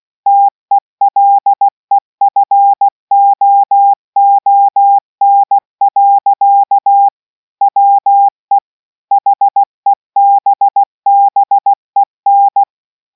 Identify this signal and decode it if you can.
Morse code, heard as a keyed tone, about 16 wpm, 800 Hz. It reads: TELEFOON. WE HEBBEN